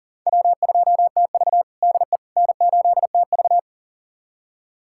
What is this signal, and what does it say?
Morse code, heard as a keyed tone, about 40 wpm, 700 Hz. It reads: W2TV DE N8TV
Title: USA Callsign DE Callsign - 4 Characters - 40wpm